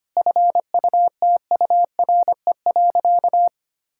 Morse code, heard as a keyed tone, about 25 wpm, 700 Hz. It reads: FUTURE.
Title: Sentences from Top 1000 Words - Review Part 6 - 25wpm